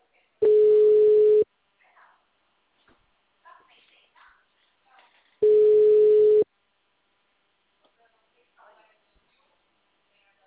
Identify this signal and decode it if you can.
Ringback tone (Europe, 425 Hz), with speech.